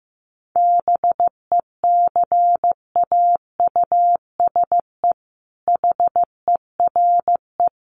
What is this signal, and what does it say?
Morse code, heard as a keyed tone, about 15 wpm, 700 Hz. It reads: BECAUSE HERE
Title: Sentences from Top 1500 Words 15wpm